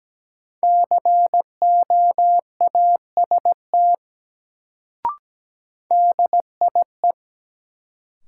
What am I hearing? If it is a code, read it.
Morse code, heard as a keyed tone, about 17 wpm, 700 Hz. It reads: COAST DIE
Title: Sentences from Top 1300 Words - New Word Drill Alt - 17wpm